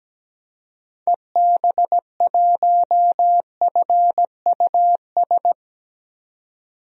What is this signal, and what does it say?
Morse code, heard as a keyed tone, about 17 wpm, 700 Hz. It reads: EB1FUS
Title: DXCC-100 Call Signs - 6 Characters - 17wpm